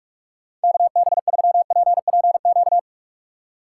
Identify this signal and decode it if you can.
Morse code, heard as a keyed tone, about 42 wpm, 700 Hz. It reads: KB3PPQ
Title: USA Call Signs - 6 Characters - 3x Character Spacing - 45wpm